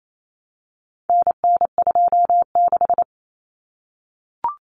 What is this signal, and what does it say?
Morse code, heard as a keyed tone, about 28 wpm, 700 Hz. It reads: NN26
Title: Common Cut Numbers - 4-Digit 28wpm